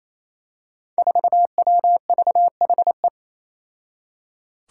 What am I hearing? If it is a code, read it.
Morse code, heard as a keyed tone, about 28 wpm, 700 Hz. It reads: 4WVHE
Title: Five Challenging Letter-Number Combinations 28wpm